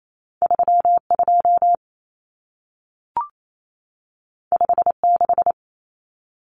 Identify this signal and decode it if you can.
Morse code, heard as a keyed tone, about 28 wpm, 700 Hz. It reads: 32 56
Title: Numbers - 2-Digit 28wpm